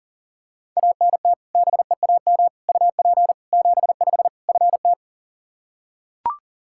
Morse code, heard as a keyed tone, about 40 wpm, 700 Hz. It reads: ANT BEAM UP 75 FT
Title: Practice QSO - Exchange 2 - Info - 40wpm